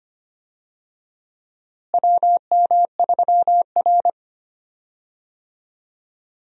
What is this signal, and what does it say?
Morse code, heard as a keyed tone, about 25 wpm, 700 Hz. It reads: WM3R
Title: USA Call Signs - 4 Characters - Repeated 3x - 25wpm